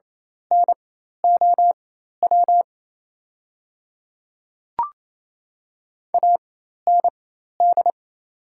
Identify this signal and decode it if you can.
Morse code, heard as a keyed tone, about 28 wpm, spaced as for 13 wpm, 690 Hz. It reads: NOW AND
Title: 4x Character Spacing - Top 200 Words - 28wpm